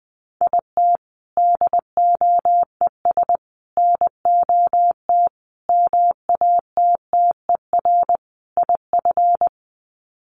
Morse code, heard as a keyed tone, about 20 wpm, 700 Hz. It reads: IT DOES NOT MATTER IF
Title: Sets of 5 Words Encore 20wpm